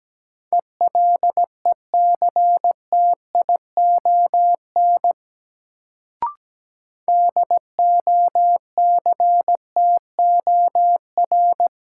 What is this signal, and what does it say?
Morse code, heard as a keyed tone, about 17 wpm, 700 Hz. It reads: ELECTION DOCTOR
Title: Sentences from Top 1500 Words - New Word Drill - 17wpm